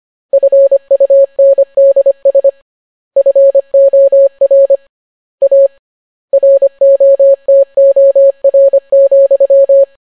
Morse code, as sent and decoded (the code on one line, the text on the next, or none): ..-. ..- -. -.. ... / ..-. --- .-. / .- / .-. --- - --- .-. --..--
FUNDS FOR A ROTOR,